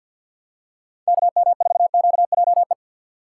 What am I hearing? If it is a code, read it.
Morse code, heard as a keyed tone, about 49 wpm, 700 Hz. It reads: KM4QJE
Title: USA Call Signs - 6 Characters - 4x Character Spacing - 50wpm